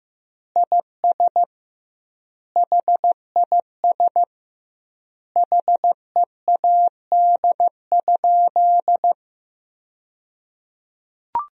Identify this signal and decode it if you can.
Morse code, heard as a keyed tone, about 15 wpm, 700 Hz. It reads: IS HIS HEAD?
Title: 2x Word-Spacing - Sentences from Top 300 Words - 15wpm